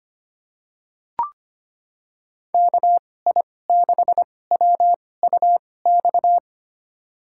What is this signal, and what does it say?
Morse code, heard as a keyed tone, about 25 wpm, spaced as for 18 wpm, 700 Hz. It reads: KI6WUX
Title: USA Call Signs - 6 Characters - 2x Character Spacing - 25wpm